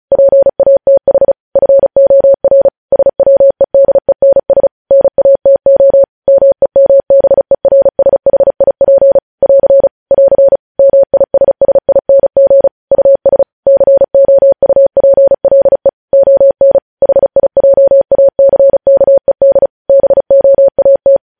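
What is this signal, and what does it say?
Morse code, heard as a keyed tone, about 36 wpm, 550 Hz. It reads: PATH FOR SWEDENS NATO MEMBERSHIP <AR> <AR> MISSING US COUPLE ON HIJACKED BOAT
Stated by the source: News headlines sent in Morse at 35 WPM.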